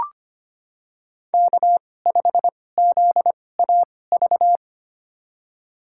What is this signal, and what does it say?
Morse code, heard as a keyed tone, about 25 wpm, spaced as for 18 wpm, 700 Hz. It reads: K5ZAV
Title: USA Call Signs - 5 Characters - 2x Character Spacing - 25wpm